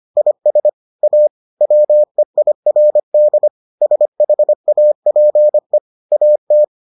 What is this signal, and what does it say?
Morse code, heard as a keyed tone, about 25 wpm, 600 Hz. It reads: IS A WEIRD SHAPE AT